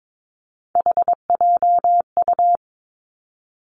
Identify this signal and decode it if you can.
Morse code, heard as a keyed tone, about 22 wpm, 700 Hz. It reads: HJU
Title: Three Challenging Letter-Number Combinations 22wpm